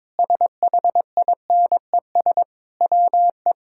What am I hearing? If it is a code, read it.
Morse code, heard as a keyed tone, about 22 wpm, 700 Hz. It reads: SHINES WE
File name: practice / 22wpm / Sentences with 4-Word Phrases - 22wpm